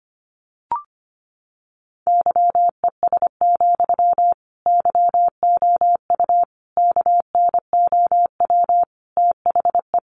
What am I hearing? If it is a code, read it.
Morse code, heard as a keyed tone, about 25 wpm, 700 Hz. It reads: YES, YOU KNOW THE
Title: Sentences from Top 700 Words 25wpm